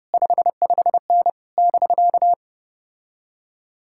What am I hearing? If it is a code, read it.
Morse code, heard as a keyed tone, about 33 wpm, 700 Hz. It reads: 55N <BK>